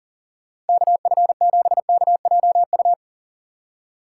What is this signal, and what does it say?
Morse code, heard as a keyed tone, about 52 wpm, spaced as for 45 wpm, 700 Hz. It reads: KF7KJU
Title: Speed Racing - USA Call Signs 40wpm